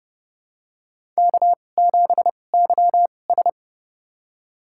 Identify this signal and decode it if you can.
Morse code, heard as a keyed tone, about 32 wpm, spaced as for 22 wpm, 700 Hz. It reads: K7YS
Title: USA Call Signs - 4 Characters - 2x Character Spacing - 30wpm